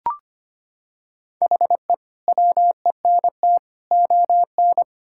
Morse code, heard as a keyed tone, about 25 wpm, 700 Hz. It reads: HE WENT ON
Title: Sentences from Top 1000 Words - Review Part 6 - 25wpm